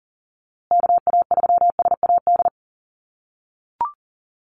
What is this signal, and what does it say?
Morse code, heard as a keyed tone, about 40 wpm, 700 Hz. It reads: KA3SAD
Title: USA Call Signs Repeated 2x 40wpm